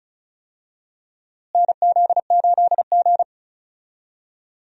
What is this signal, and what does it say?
Morse code, heard as a keyed tone, about 34 wpm, 700 Hz. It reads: NZ8G
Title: USA Call Signs - 4 Characters - Repeated 3x - 35wpm